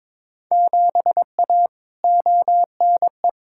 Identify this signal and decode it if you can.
Morse code, heard as a keyed tone, about 21 wpm, 700 Hz. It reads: 7A ONE
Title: Field Day Exchanges Encore - 22wpm